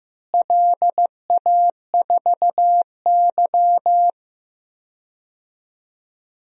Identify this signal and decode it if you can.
Morse code, heard as a keyed tone, about 15 wpm, 700 Hz. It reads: LA4Y
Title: DXCC-100 Call Signs - 4 Characters - Repeated 3x - 15wpm